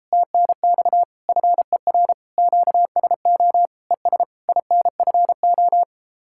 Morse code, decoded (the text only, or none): TNX FER QSO ES INFO